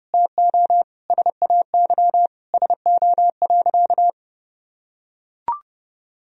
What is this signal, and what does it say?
Morse code, heard as a keyed tone, about 32 wpm, 700 Hz. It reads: TO SAY SO.
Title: Sentences from Top 1000 Words - Review Part 1 - 30wpm